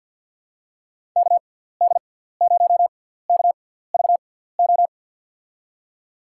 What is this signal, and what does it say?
Morse code, heard as a keyed tone, about 48 wpm, spaced as for 17 wpm, 700 Hz. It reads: KD0KVO
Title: USA Call Signs - 6 Characters - 6x Character Spacing - 50wpm